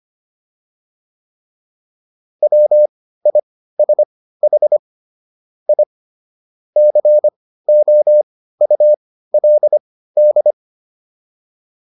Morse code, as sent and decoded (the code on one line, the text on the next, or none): .-- .. ... .... / .. / -.-. --- ..- .-.. -..
WISH I COULD